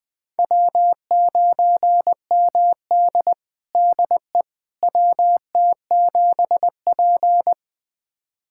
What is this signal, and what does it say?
Morse code, heard as a keyed tone, about 20 wpm, 700 Hz. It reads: W9MD DE WT7P
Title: USA Callsign DE Callsign - 4 Characters - 20wpm